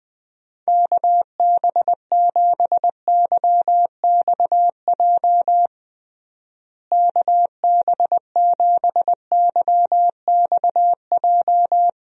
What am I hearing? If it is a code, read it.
Morse code, heard as a keyed tone, about 20 wpm, 700 Hz. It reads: KB7YXJ KB7YXJ
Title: USA Call Signs - 6 Characters - Repeated 2x - 20wpm